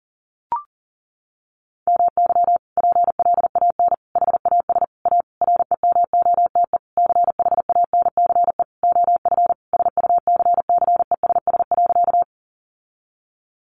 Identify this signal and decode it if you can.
Morse code, heard as a keyed tone, about 40 wpm, 700 Hz. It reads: MY PLAN HAS A REMOTE CHANCE OF SUCCESS.
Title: Sentences from Top 1300 Words 40wpm